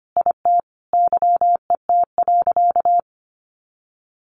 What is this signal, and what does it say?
Morse code, heard as a keyed tone, about 25 wpm, 700 Hz. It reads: IT YET.
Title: Sentences from Top 1100 Words 25wpm